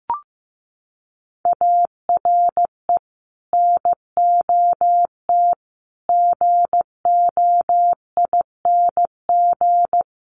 Morse code, heard as a keyed tone, about 15 wpm, 700 Hz. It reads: ARE NOT GOING
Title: Sets of 4 Words 15wpm